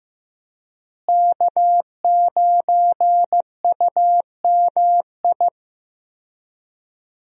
Morse code, heard as a keyed tone, about 15 wpm, 700 Hz. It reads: K9UMI
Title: USA Call Signs - 5 Characters - Repeated 3x - 15wpm